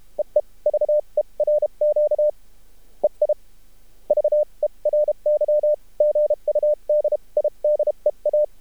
Morse code, heard as a keyed tone, about 32 wpm, spaced as for 27 wpm, 600 Hz. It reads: EE VERQ EI VERY GUD IDEA
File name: archived / CRSnet